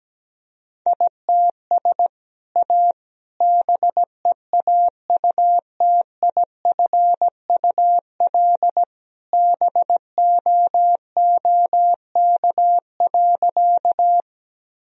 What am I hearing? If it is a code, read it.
Morse code, heard as a keyed tone, about 17 wpm, 700 Hz. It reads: ITS A BEAUTIFUL BOOK.
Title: Sentences from Top 1000 Words - Review Part 1 - 17wpm